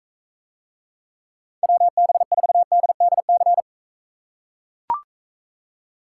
Morse code, heard as a keyed tone, about 43 wpm, 700 Hz. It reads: WB4DDC